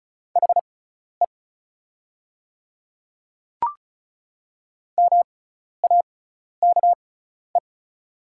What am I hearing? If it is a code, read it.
Morse code, heard as a keyed tone, about 36 wpm, spaced as for 12 wpm, 690 Hz. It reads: HE MAKE
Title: 6x Character Spacing - Top 200 Words - 35wpm